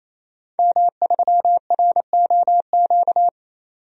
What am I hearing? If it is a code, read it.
Morse code, heard as a keyed tone, about 28 wpm, 700 Hz. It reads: M3ROQ
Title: DXCC-100 Call Signs - 5 Characters - 28wpm